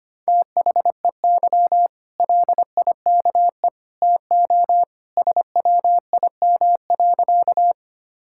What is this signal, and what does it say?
Morse code, heard as a keyed tone, about 25 wpm, 700 Hz. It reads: THEY LIKE TO SWIM.
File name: Easy Sentences with Hints - 25wpm